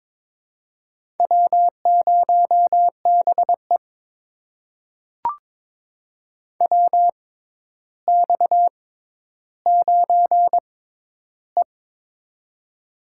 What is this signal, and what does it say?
Morse code, heard as a keyed tone, about 22 wpm, spaced as for 11 wpm, 700 Hz. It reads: W0BE WX9E
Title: USA Call Signs - 4 Characters - 6x Character Spacing - 22wpm